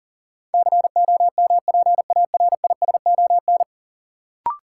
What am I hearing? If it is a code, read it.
Morse code, heard as a keyed tone, about 40 wpm, 700 Hz. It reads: COMPARISON
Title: Sentences from Top 1600 Words - New Word Drill - 40wpm